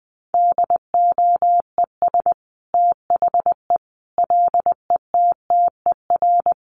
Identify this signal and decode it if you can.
Morse code, heard as a keyed tone, about 20 wpm, 700 Hz. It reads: DOES THE LETTER
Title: Sentences from Top 1100 Words 20wpm